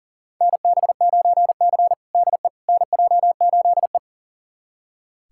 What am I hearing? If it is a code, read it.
Morse code, heard as a keyed tone, about 37 wpm, 700 Hz. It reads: NB9C DE NJ8E
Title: USA Callsign DE Callsign - 4 Characters - 40wpm